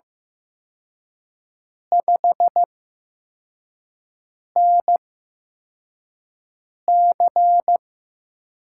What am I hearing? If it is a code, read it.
Morse code, heard as a keyed tone, about 15 wpm, spaced as for 4 wpm, 700 Hz. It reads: 5NC
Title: USA Call Signs - 6 Characters - 8x Character Spacing - 15wpm